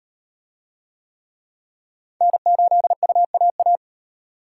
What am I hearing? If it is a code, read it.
Morse code, heard as a keyed tone, about 38 wpm, 700 Hz. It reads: N8UAA